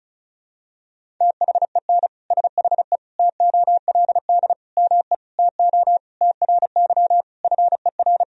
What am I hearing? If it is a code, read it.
Morse code, heard as a keyed tone, about 36 wpm, 700 Hz. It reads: THEN SHE TOLD ME TO TRY FER
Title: Sentences from Top 400 Words and Abbreviations 35wpm